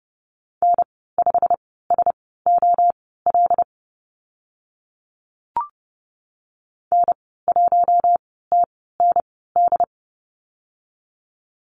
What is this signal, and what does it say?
Morse code, heard as a keyed tone, about 30 wpm, spaced as for 17 wpm, 700 Hz. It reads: N5SOL N1TND
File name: Lesson 15g - Number 5 - Call Signs 30wpm